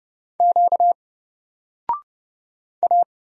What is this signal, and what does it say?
Morse code, heard as a keyed tone, about 30 wpm, 700 Hz. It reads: Q A
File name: Single-Letter-Number-30wpm